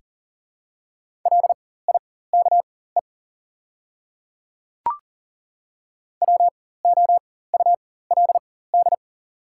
Morse code, heard as a keyed tone, about 41 wpm, spaced as for 19 wpm, 690 Hz. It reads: LIKE WOULD
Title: 4x Character Spacing - Top 100 Words - 40wpm